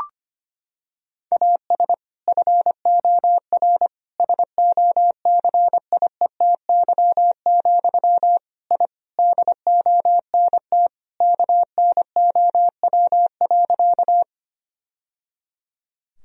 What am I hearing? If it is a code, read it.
Morse code, heard as a keyed tone, about 25 wpm, 700 Hz. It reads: AS FOR SOCIETY, I DONT KNOW.
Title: Sentences from Top 1000 Words - Review Part 4 - 25wpm